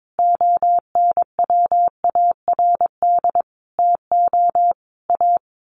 Morse code, heard as a keyed tone, about 22 wpm, 700 Hz. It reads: ONWARD TO A
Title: A Sound Of Thunder 22wpm